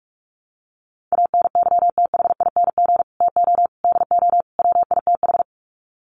Morse code, heard as a keyed tone, about 44 wpm, 700 Hz. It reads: ANYTHING TO DO WITH
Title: Sets of 4 Words 45wpm